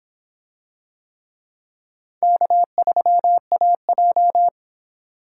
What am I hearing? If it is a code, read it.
Morse code, heard as a keyed tone, about 25 wpm, 700 Hz. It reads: K3AJ